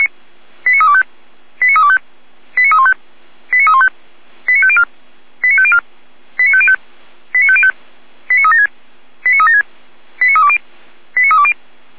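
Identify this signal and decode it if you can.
Five-tone selective call (ZVEI), heard as quick runs of five tones, tones of about 70 ms. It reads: 89325 89325 89315 89315 89504 89504 89505 89505 89376 89376 89320 89320
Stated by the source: Five-tone ZVEI selective call received on an SDR.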